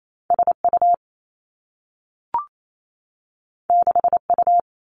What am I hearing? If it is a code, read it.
Morse code, heard as a keyed tone, about 28 wpm, 700 Hz. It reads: S U 6 U
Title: Two Challenging Letter-Number Combinations 28wpm